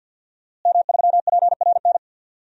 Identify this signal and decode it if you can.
Morse code, heard as a keyed tone, about 52 wpm, 700 Hz. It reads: M3PRN
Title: DXCC-100 Call Signs - 5 Characters - Repeated 2x - 50wpm